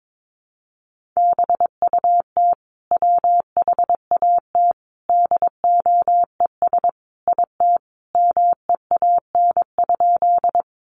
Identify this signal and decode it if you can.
Morse code, heard as a keyed tone, about 22 wpm, 700 Hz. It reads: BUT WHAT DOES IT MEAN?
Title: Sentences from Top 200 Words 22wpm